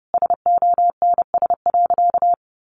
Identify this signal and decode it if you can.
Morse code, heard as a keyed tone, about 30 wpm, 700 Hz. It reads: SONS.
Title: A Sound Of Thunder 30wpm